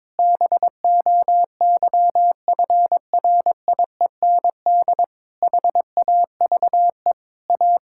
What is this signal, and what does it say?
Morse code, heard as a keyed tone, about 22 wpm, 700 Hz. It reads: BOYFRIEND HAVE A
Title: Sentences from Top 1000 Words - Review Part 5 - 22wpm